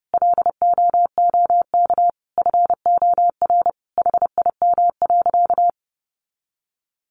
Morse code, heard as a keyed tone, about 30 wpm, 700 Hz. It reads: LOOK FOR HIM.
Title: Sentences from Top 1100 Words 30wpm